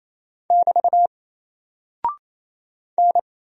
Morse code, heard as a keyed tone, about 27 wpm, 700 Hz. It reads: <BT> N
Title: Single Character (Letter:Number:Punctuation:Prosign) 28wpm